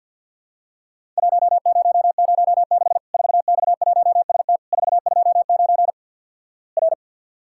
Morse code, heard as a keyed tone, about 50 wpm, 700 Hz. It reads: 1006 4X1ST 429 R
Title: Worked All Europe DX Contest - QTC Exchange with Repeats - 50wpm